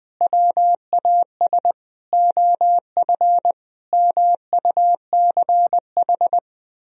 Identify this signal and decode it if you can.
Morse code, heard as a keyed tone, about 20 wpm, 700 Hz. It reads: WAS OF MUCH
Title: Sentences from Top 1000 Words - Review Part 3 - 20wpm